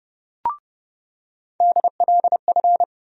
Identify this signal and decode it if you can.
Morse code, heard as a keyed tone, about 30 wpm, 700 Hz. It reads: DLF
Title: USA Call Sign Postfixes - 3 Letters 30wpm